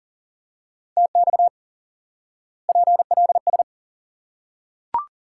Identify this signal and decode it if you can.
Morse code, heard as a keyed tone, about 40 wpm, 700 Hz. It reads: TX PLS
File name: IQE Course 3 - Lesson 89 - YAGI - Two Elements - 40wpm